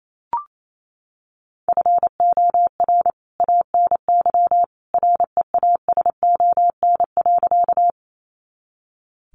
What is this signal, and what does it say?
Morse code, heard as a keyed tone, about 28 wpm, 700 Hz. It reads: FOR ANY REASON.
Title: A Sound Of Thunder 28wpm